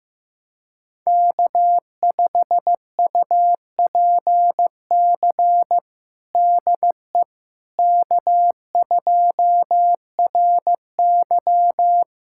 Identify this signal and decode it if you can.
Morse code, heard as a keyed tone, about 15 wpm, 700 Hz. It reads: K5UPC DE K2RY
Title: USA Callsign DE Callsign 15wpm